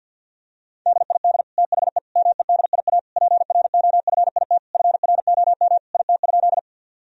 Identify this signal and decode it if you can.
Morse code, heard as a keyed tone, about 47 wpm, 700 Hz. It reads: DID THE MEDIA PROFIT FROM IT?
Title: Sentences from Top 1000 Words - Review Part 4 - 50wpm